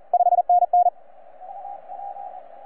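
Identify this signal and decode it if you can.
Morse code, heard as a keyed tone, about 33 wpm, 720 Hz. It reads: 5NN
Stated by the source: Z81X on 15 m CW, from DX World.net.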